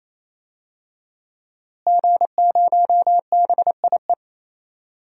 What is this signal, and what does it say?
Morse code, heard as a keyed tone, about 28 wpm, 700 Hz. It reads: G0BIE
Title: DXCC-100 Call Signs - 5 Characters - Repeated 3x - 28wpm